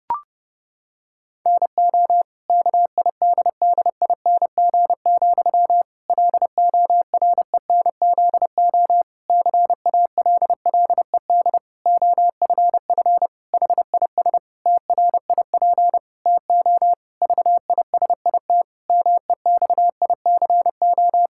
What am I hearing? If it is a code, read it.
Morse code, heard as a keyed tone, about 30 wpm, 700 Hz. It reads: NO KIDDING, LORENZO CALLED OFF HIS TRIP TO VISIT MEXICO
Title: Pangram Challenge v2 - 30wpm